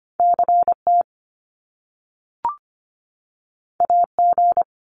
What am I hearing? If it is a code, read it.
Morse code, heard as a keyed tone, about 25 wpm, 700 Hz. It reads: CT AG
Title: Top Two Letter Combinations in Words 25wpm